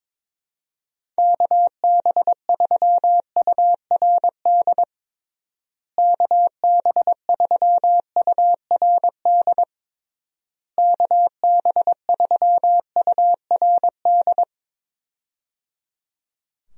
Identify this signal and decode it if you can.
Morse code, heard as a keyed tone, about 22 wpm, 700 Hz. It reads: KB3URD KB3URD KB3URD
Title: USA Call Signs - 6 Characters - Repeated 3x - 22wpm